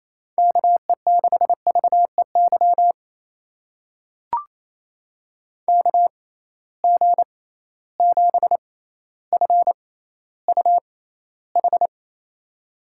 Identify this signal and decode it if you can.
Morse code, heard as a keyed tone, about 28 wpm, spaced as for 14 wpm, 700 Hz. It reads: KE6VEY KG7FUH
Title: USA Call Signs - 6 Characters - 6x Character Spacing - 28wpm